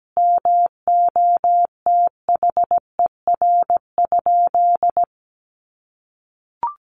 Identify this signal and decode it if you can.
Morse code, heard as a keyed tone, about 17 wpm, 700 Hz. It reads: MOTHER?
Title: Sentences from Top 200 Words 17wpm